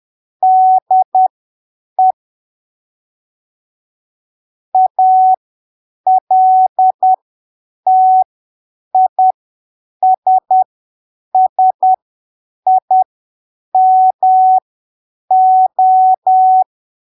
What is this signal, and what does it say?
Morse code, heard as a keyed tone, about 10 wpm, spaced as for 7 wpm, 750 Hz. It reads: DE ALTISSIMO
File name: arquivos / texto / texto 11 10ppm